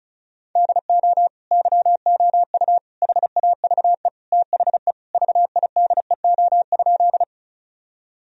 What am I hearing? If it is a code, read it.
Morse code, heard as a keyed tone, about 36 wpm, 700 Hz. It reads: DO YOU HAVE THE VIDEO?
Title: Sentences from Top 1000 Words - Review Part 4 - 35wpm